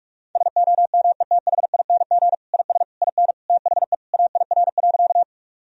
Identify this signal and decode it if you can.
Morse code, heard as a keyed tone, about 42 wpm, 700 Hz. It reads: SOMETHING IS IN THE AIR.
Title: Sentences from Top 1000 Words - Review Part 1 - 45wpm